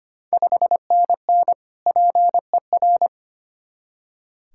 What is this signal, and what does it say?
Morse code, heard as a keyed tone, about 25 wpm, 700 Hz. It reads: 5NN PER
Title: Ontario QSO Party Exchange 25wpm